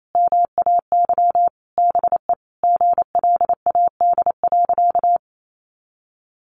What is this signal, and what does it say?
Morse code, heard as a keyed tone, about 28 wpm, 700 Hz. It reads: MAY BE GLAD.
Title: Sentences from Top 800 Words 28wpm